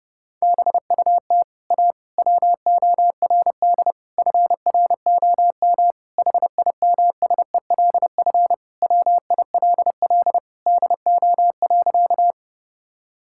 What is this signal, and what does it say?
Morse code, heard as a keyed tone, about 30 wpm, 700 Hz. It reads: BUT A WORD FROM HIMSELF WILL DO.
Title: Sentences from Top 1600 Words 30wpm